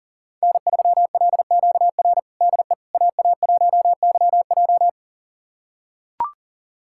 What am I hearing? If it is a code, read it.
Morse code, heard as a keyed tone, about 40 wpm, 700 Hz. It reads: N3LQR DE AA1YJ
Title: USA Callsign DE Callsign - 5 Characters - 40wpm